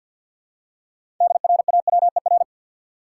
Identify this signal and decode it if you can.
Morse code, heard as a keyed tone, about 52 wpm, 700 Hz. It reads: DRAWER
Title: Six Letter Words - 50wpm